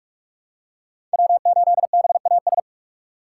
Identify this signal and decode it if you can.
Morse code, heard as a keyed tone, about 46 wpm, 700 Hz. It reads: W8BAS
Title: USA Call Signs - 5 Characters - 3x Character Spacing - 45wpm